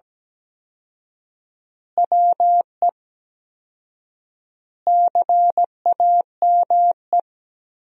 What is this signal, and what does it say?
Morse code, heard as a keyed tone, about 17 wpm, 700 Hz. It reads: WE CAME
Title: 4x Word-Spacing - Sentences from Top 400 Words - 17wpm